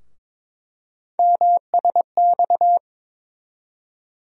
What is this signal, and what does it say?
Morse code, heard as a keyed tone, about 22 wpm, 700 Hz. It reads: MSX
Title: USA Call Sign Postfixes - 3 Letters 22wpm